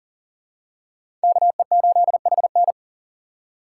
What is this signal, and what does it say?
Morse code, heard as a keyed tone, about 40 wpm, 700 Hz. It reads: KE8HN
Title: USA Call Signs - 5 Characters - 4x Character Spacing - 40wpm